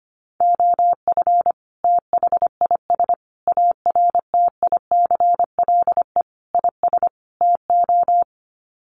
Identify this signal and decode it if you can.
Morse code, heard as a keyed tone, about 25 wpm, 700 Hz. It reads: OF THIS ARTICLE IS TO
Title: Sets of 5 Words Encore2 25wpm